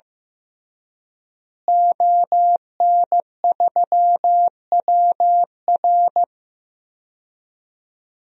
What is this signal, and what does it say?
Morse code, heard as a keyed tone, about 16 wpm, 700 Hz. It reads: ON3WR
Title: DXCC-100 Call Signs - 5 Characters - Repeated 3x - 15wpm